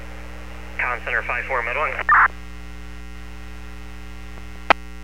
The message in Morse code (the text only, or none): none